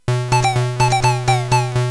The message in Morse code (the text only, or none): none